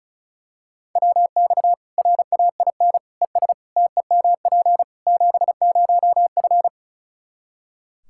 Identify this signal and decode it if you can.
Morse code, heard as a keyed tone, about 36 wpm, 700 Hz. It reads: WX RAIN ES TEMP 70F
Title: Practice QSO - Exchange 2 - Info - 35wpm